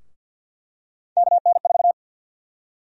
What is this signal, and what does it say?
Morse code, heard as a keyed tone, about 52 wpm, 700 Hz. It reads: KN4
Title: USA Call Sign Prefixes - 50wpm